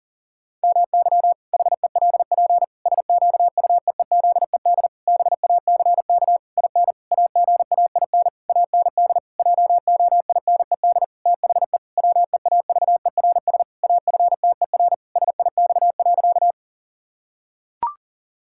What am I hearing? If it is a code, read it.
Morse code, heard as a keyed tone, about 40 wpm, 700 Hz. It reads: MY HELP SQUEEZED BACK IN AGAIN AND JOINED THE WEAVERS AFTER SIX.
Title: Pangram Challenge v2 - 40wpm